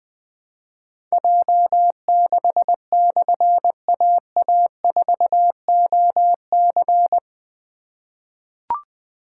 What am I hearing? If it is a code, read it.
Morse code, heard as a keyed tone, about 21 wpm, 700 Hz. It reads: J6/AA4OC
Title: Difficult Call Signs 20wpm